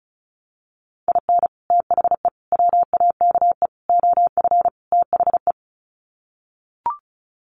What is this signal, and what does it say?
Morse code, heard as a keyed tone, about 34 wpm, 700 Hz. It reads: IN THE WAKE OF THE
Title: Sets of 5 Words 35wpm